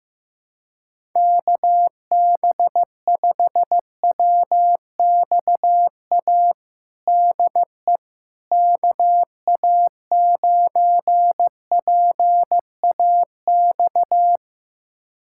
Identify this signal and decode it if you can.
Morse code, heard as a keyed tone, about 15 wpm, 700 Hz. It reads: KB5WXA DE KA9PAX
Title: USA Callsign DE Callsign - 6 Characters - 15wpm